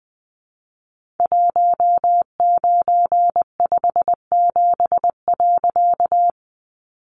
Morse code, heard as a keyed tone, about 20 wpm, 700 Hz. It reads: 1957.
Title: A Sound Of Thunder 20wpm